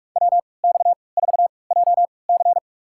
Morse code, heard as a keyed tone, about 42 wpm, spaced as for 25 wpm, 700 Hz. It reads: WX4JC
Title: USA Call Signs - 5 Characters - 3x Character Spacing - 45wpm